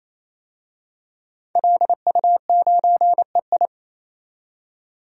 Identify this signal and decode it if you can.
Morse code, heard as a keyed tone, about 28 wpm, 700 Hz. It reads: LU9EI
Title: DXCC-100 Call Signs - 5 Characters - 28wpm